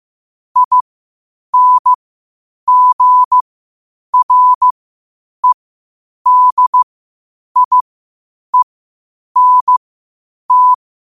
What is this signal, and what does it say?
Morse code, heard as a keyed tone, about 15 wpm, spaced as for 8 wpm, 1000 Hz. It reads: INGREDIENT
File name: qotdfiles / 8wpm